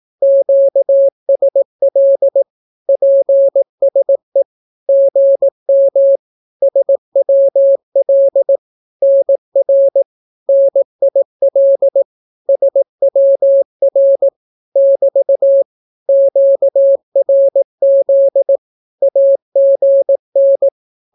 Morse code, heard as a keyed tone, about 18 wpm, 550 Hz. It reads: QSL PSE GM SWL NR NIL SWR <BT> QRZ AGN
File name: naukapodstaw / skroty / skroty